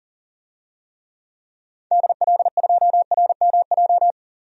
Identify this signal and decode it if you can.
Morse code, heard as a keyed tone, about 38 wpm, 700 Hz. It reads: DL2RMJ
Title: DXCC-100 Call Signs - 6 Characters - 40wpm